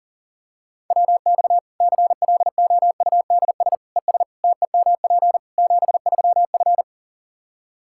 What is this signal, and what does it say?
Morse code, heard as a keyed tone, about 40 wpm, 700 Hz. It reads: WX CLOUDS ES TEMP 73F
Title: Practice QSO - Exchange 2 - Info - 40wpm